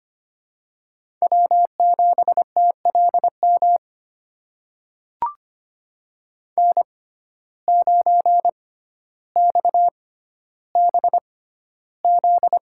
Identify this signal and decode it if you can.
Morse code, heard as a keyed tone, about 25 wpm, spaced as for 13 wpm, 700 Hz. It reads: W7TLM N9XBZ
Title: USA Call Signs - 5 Characters - 6x Character Spacing - 25wpm